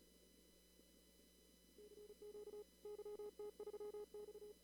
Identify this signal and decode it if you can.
Morse code, heard as a keyed tone, about 33 wpm, 420 Hz. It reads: CQ YT3X
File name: CW / kt5j wpx cw-op-k5pi